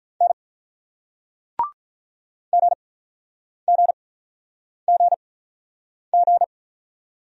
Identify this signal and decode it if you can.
Morse code, heard as a keyed tone, about 47 wpm, spaced as for 9 wpm, 700 Hz. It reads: N GGGG